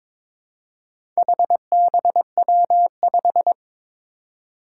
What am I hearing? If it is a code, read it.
Morse code, heard as a keyed tone, about 22 wpm, 700 Hz. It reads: HBW5
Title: Four Challenging Letter-Number Combinations 22wpm